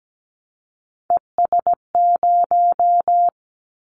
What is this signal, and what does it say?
Morse code, heard as a keyed tone, about 17 wpm, 700 Hz. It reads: ES0
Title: Callsign Prefixes for Top 100 DXCC - 17wpm